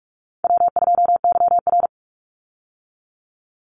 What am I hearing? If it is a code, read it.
Morse code, heard as a keyed tone, about 44 wpm, 700 Hz. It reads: W2YR
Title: Easy Call Signs Repeated 2x - 45wpm